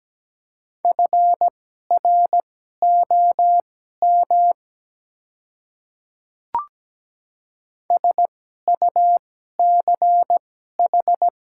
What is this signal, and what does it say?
Morse code, heard as a keyed tone, about 17 wpm, spaced as for 12 wpm, 700 Hz. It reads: FROM SUCH
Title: 2x Character Spacing - Top 300 Words - 17wpm